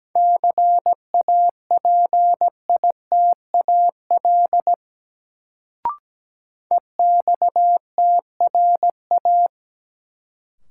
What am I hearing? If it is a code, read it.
Morse code, heard as a keyed tone, about 17 wpm, 700 Hz. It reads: CAPITAL EXTRA
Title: Sentences from Top 1000 Words - New Word Drill - 17wpm